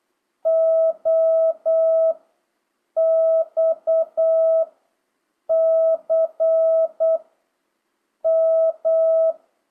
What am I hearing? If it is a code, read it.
Morse code, heard as a keyed tone, about 8 wpm, 660 Hz. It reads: O X C M